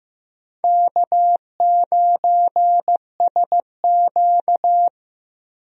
Morse code, heard as a keyed tone, about 15 wpm, 700 Hz. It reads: K9SQ
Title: USA Call Signs - 4 Characters - 15wpm